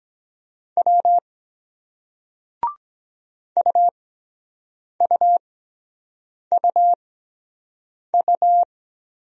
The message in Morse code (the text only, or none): W UUUU